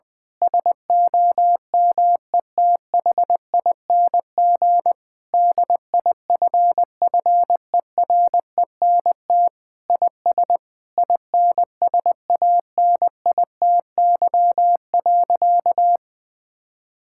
Morse code, heard as a keyed tone, about 20 wpm, 700 Hz. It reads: SOMETHING DIFFERENT IS INSANITY.
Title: Sentences with 8-Word Phrases - 20wpm